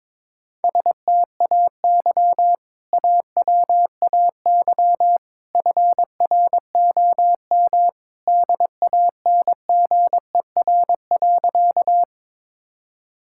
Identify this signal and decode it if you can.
Morse code, heard as a keyed tone, about 22 wpm, 700 Hz. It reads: STAY AWAY FROM DANGER.
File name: practice / 22wpm / Sentences with 2-Word Phrases - 22wpm